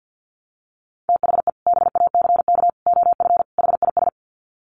Morse code, heard as a keyed tone, about 50 wpm, 700 Hz. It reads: THE BACK OF HIS
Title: Sets of 4 Words 50wpm